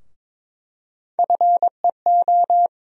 Morse code, heard as a keyed tone, about 22 wpm, 700 Hz. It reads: FEO
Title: USA Call Sign Postfixes - 3 Letters 22wpm